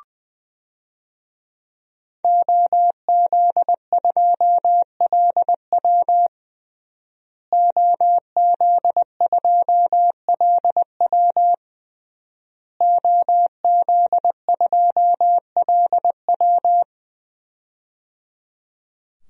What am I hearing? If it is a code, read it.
Morse code, heard as a keyed tone, about 21 wpm, 700 Hz. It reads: OZ2LW OZ2LW OZ2LW
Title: DXCC-100 Call Signs - 5 Characters - Repeated 3x - 20wpm